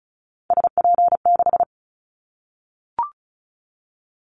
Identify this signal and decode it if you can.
Morse code, heard as a keyed tone, about 34 wpm, 700 Hz. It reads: SP6
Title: Callsign Prefixes for Top 25 DXCC - Repeated 3x - 35wpm